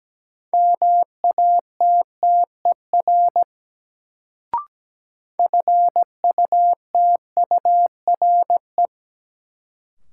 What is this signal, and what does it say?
Morse code, heard as a keyed tone, about 17 wpm, 700 Hz. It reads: MATTER FUTURE
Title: Sentences from Top 700 Words - New Word Drill - 17wpm